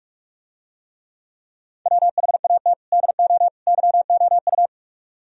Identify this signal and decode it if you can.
Morse code, heard as a keyed tone, about 46 wpm, 700 Hz. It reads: WHAT DO YOU